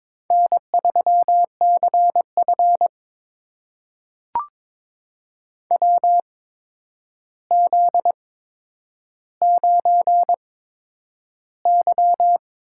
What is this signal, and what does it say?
Morse code, heard as a keyed tone, about 22 wpm, spaced as for 9 wpm, 700 Hz. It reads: N3CF WZ9Y
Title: USA Call Signs - 4 Characters - 8x Character Spacing - 22wpm